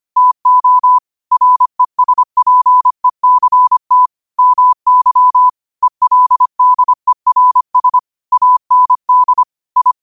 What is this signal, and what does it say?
Morse code, heard as a keyed tone, about 25 wpm, 1000 Hz. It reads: TO RESPECT MY ELDERS AND I
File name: 25wpm